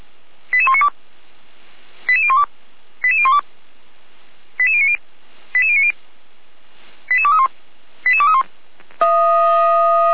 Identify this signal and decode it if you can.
Five-tone selective call (ZVEI), heard as quick runs of five tones, tones of about 70 ms. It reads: 88192 88012 88012 88090 88090 88231 88231